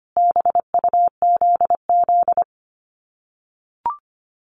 Morse code, heard as a keyed tone, about 25 wpm, 700 Hz. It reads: BUZZ
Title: Common QSO Phrases 25wpm